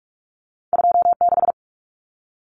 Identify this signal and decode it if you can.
Morse code, heard as a keyed tone, about 46 wpm, 700 Hz. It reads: 26
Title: Numbers - 2-Digit 45wpm